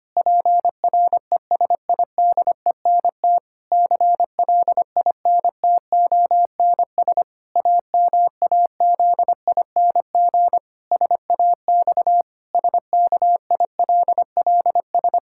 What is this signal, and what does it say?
Morse code, heard as a keyed tone, about 25 wpm, 700 Hz. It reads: PRESIDENT CLINTONS AMAZING SAX SKILLS
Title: Pangram Challenge v2 - 25wpm